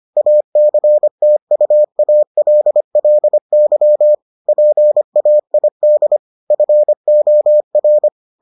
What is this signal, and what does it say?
Morse code, heard as a keyed tone, about 25 wpm, 600 Hz. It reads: ACTUALLY PAID FOR